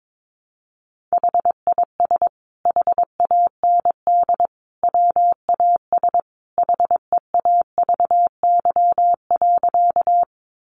Morse code, heard as a keyed tone, about 22 wpm, 700 Hz. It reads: HIS HAND WAS HEAVY.
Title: Sentences from Top 1100 Words 22wpm